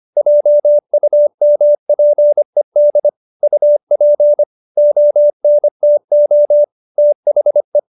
Morse code, heard as a keyed tone, about 25 wpm, 590 Hz. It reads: JUMPED UP ONTO THE